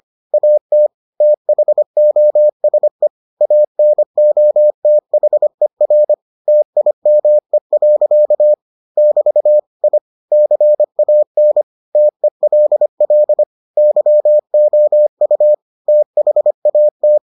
Morse code, heard as a keyed tone, about 25 wpm, 600 Hz. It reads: AT THOSE ANOTHER TIME. <BT> I CAN TELL YOU THAT